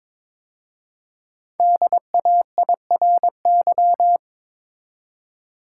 Morse code, heard as a keyed tone, about 22 wpm, 700 Hz. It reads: DAIRY
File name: 5 Letter Words - First 3 Letters Given - 22wpm